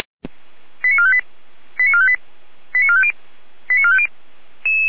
Five-tone selective call (ZVEI), heard as quick runs of five tones, tones of about 70 ms. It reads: 89458 89458 89450 89450
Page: SDR/ZVEI